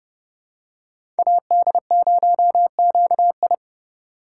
Morse code, heard as a keyed tone, about 30 wpm, 700 Hz. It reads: AD0QI
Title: USA Call Signs - 2x Character Spacing - 30wpm